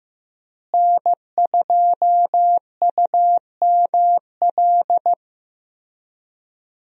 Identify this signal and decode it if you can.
Morse code, heard as a keyed tone, about 15 wpm, 700 Hz. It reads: N2UML
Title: USA Call Signs - 5 Characters - Repeated 3x - 15wpm